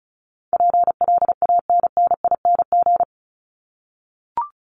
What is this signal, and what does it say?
Morse code, heard as a keyed tone, about 34 wpm, 700 Hz. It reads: PLANNING
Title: ING Suffix - Top 100 Words with Base Words 35wpm